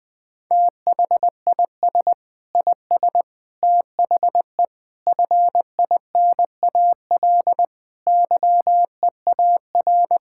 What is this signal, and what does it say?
Morse code, heard as a keyed tone, about 20 wpm, 700 Hz. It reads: THIS IS THE FINAL YEAR
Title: Sentences with 6-Word Phrases - 20wpm